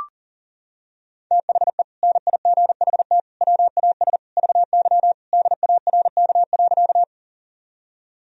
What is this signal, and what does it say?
Morse code, heard as a keyed tone, about 40 wpm, 700 Hz. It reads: THE NIGHT WAS VY DARK.
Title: Sentences from Top 500 Words and Abbreviations 40wpm